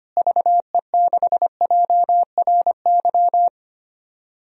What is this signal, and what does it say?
Morse code, heard as a keyed tone, about 25 wpm, 700 Hz. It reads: VE6JRY
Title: DXCC-100 Call Signs - 6 Characters - Repeated 3x - 25wpm